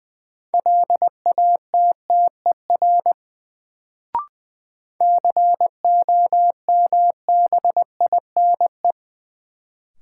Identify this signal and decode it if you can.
Morse code, heard as a keyed tone, about 21 wpm, 700 Hz. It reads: LATTER COMBINE
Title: Sentences from Top 1600 Words - New Word Drill Alt - 20wpm